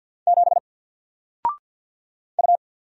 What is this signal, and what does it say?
Morse code, heard as a keyed tone, about 50 wpm, 700 Hz. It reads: 7 U
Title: Single Letter-Number Alt 50wpm